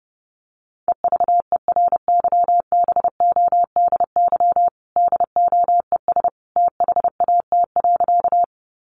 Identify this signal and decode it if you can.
Morse code, heard as a keyed tone, about 30 wpm, 700 Hz. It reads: EVERYBODY DOES THAT.
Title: Sentences from Top 1400 Words 30wpm